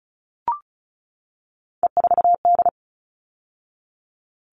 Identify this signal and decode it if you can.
Morse code, heard as a keyed tone, about 34 wpm, 700 Hz. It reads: E4D
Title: All Cut Numbers - 3-Digit 35wpm